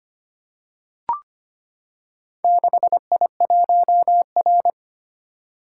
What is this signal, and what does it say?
Morse code, heard as a keyed tone, about 25 wpm, 700 Hz. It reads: 6I1R
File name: practice / 25wpm / Four Letter-Number Combinations - 25wpm